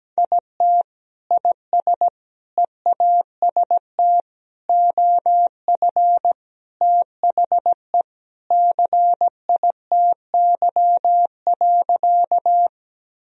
Morse code, heard as a keyed tone, about 17 wpm, 700 Hz. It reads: IT IS EAST OF THE CITY.
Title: Sentences from Top 1000 Words - Review Part 3 - 17wpm